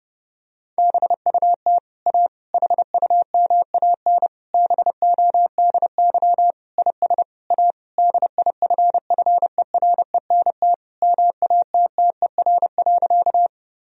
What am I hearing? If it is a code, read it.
Morse code, heard as a keyed tone, about 30 wpm, 700 Hz. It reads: BUT A HUMAN BODY IS A DIFFERENT MATTER.
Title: Sentences from Top 1000 Words - Review Part 3 - 30wpm